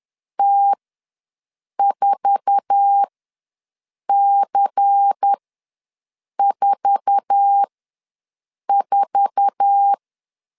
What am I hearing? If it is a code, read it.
Morse code, heard as a keyed tone, about 10 wpm, spaced as for 6 wpm, 800 Hz. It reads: T4C44
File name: LEZIONI / 8 YJ45 / 8 YJ4 ws 10wpm 4020